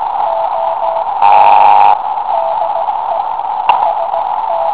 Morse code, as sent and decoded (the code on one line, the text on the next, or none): none